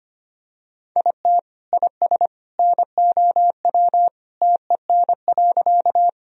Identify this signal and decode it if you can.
Morse code, heard as a keyed tone, about 25 wpm, 700 Hz. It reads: IT IS NOW TEN.
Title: Sentences from Top 1000 Words - Review Part 2 - 25wpm